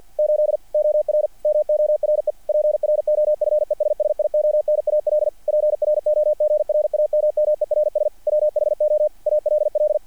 Morse code, heard as a keyed tone, about 48 wpm, 660 Hz. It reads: 8 OR MORE PROFESSIONAL PROGRAMMERS WHO ALL
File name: archived / CRSnet